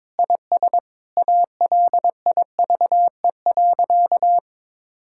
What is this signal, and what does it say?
Morse code, heard as a keyed tone, about 22 wpm, 700 Hz. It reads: IS ALIVE.
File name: practice / 22wpm / Sentences with 2-Word Phrases - 22wpm